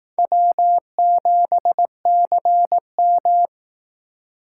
Morse code, heard as a keyed tone, about 18 wpm, 700 Hz. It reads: W7CM
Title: Speed Racing - USA Call Signs 15wpm